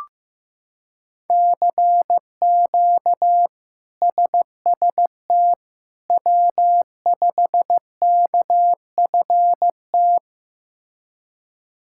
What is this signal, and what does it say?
Morse code, heard as a keyed tone, about 15 wpm, 700 Hz. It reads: CQ SST W5KFT
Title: K1USN SST QSO Practice - 15wpm